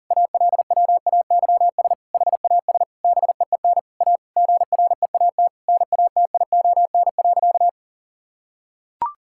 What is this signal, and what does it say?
Morse code, heard as a keyed tone, about 37 wpm, 700 Hz. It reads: ALWAYS HAS BEEN A GREAT NATION.
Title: Sentences from Top 1000 Words - Review Part 6 - 40wpm